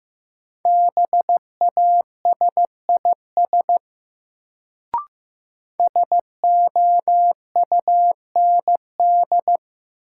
Morse code, heard as a keyed tone, about 15 wpm, 700 Hz. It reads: BASIS SOUND
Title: Sets of 3 Words - New Word Drill - 15wpm